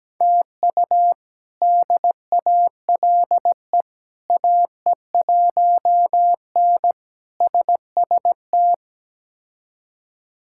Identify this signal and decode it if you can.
Morse code, heard as a keyed tone, about 17 wpm, 700 Hz. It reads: TU DALE AE1N SST
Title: K1USN SST QSO Practice - 17wpm